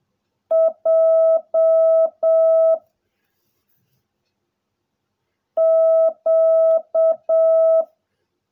Morse code, heard as a keyed tone, about 7 wpm, 660 Hz. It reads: J Q
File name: jh8cbh / 2021cw / cw16a